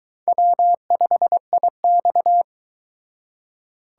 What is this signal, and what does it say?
Morse code, heard as a keyed tone, about 22 wpm, 700 Hz. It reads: W5IX